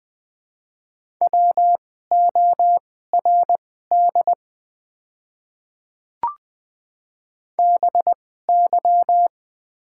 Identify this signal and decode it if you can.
Morse code, heard as a keyed tone, about 20 wpm, spaced as for 14 wpm, 700 Hz. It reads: WORD BY
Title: 2x Character Spacing - Top 100 Words - 20wpm